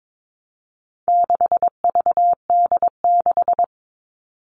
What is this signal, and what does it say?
Morse code, heard as a keyed tone, about 22 wpm, 700 Hz. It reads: 6VD6
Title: All Cut Numbers - 4-Digit 22wpm